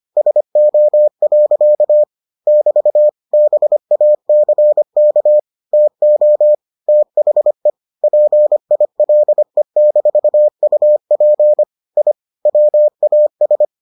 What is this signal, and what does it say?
Morse code, heard as a keyed tone, about 25 wpm, 600 Hz. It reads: SO. <BT> BACK TO THE PILE-UP I WAS